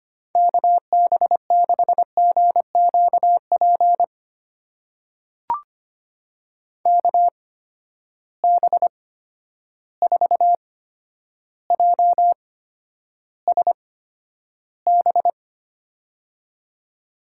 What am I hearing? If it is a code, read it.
Morse code, heard as a keyed tone, about 25 wpm, spaced as for 11 wpm, 700 Hz. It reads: KB6GQP KB4JSB
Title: USA Call Signs - 6 Characters - 8x Character Spacing - 25wpm